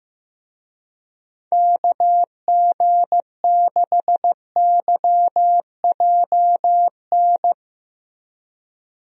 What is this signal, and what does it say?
Morse code, heard as a keyed tone, about 16 wpm, 700 Hz. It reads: KG6YJN
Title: USA Call Signs - 6 Characters - Repeated 2x - 15wpm